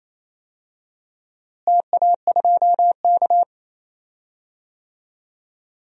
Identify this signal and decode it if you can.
Morse code, heard as a keyed tone, about 28 wpm, 700 Hz. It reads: TA2K
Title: DXCC-100 Call Signs - 4 Characters - Repeated 2x - 28wpm